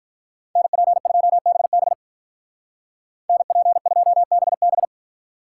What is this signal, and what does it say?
Morse code, heard as a keyed tone, about 45 wpm, 700 Hz. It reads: NP2BB NP2BB